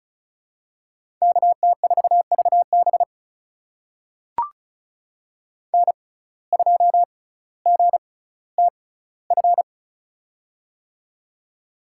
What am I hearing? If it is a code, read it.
Morse code, heard as a keyed tone, about 36 wpm, spaced as for 18 wpm, 700 Hz. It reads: KT4VB N2GTF
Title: USA Call Signs - 5 Characters - 6x Character Spacing - 35wpm